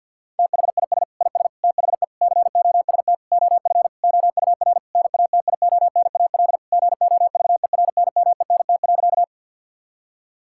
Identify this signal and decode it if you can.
Morse code, heard as a keyed tone, about 50 wpm, 690 Hz. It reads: THIS IS THE COST OF OUR NATIONAL GOVERNMENT.
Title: Sentences from Top 1000 Words - Review Part 5 - 50wpm